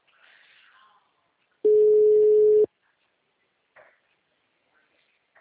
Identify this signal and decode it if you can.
Ringback tone (Europe, 425 Hz).